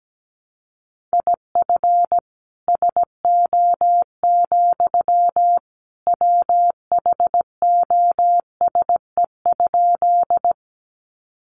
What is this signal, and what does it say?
Morse code, heard as a keyed tone, about 17 wpm, 700 Hz. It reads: IF SO, WHOSE?
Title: Sentences from Top 1200 Words 17wpm